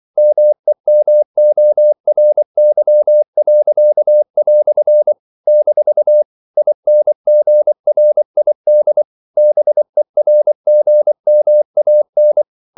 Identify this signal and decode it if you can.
Morse code, heard as a keyed tone, about 24 wpm, 600 Hz. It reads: MEMORY." - INGRID BERGMAN
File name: CW / Practice / EnglishQuotations0324